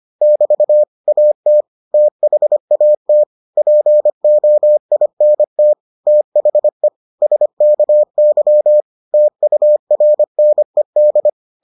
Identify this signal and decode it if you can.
Morse code, heard as a keyed tone, about 25 wpm, 600 Hz. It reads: <BT> AT THAT POINT THE SKY TURNED